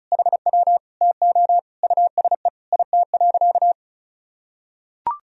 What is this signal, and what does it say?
Morse code, heard as a keyed tone, about 36 wpm, 700 Hz. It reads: HW TO USE IT.
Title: Sentences from Top 500 Words and Abbreviations 35wpm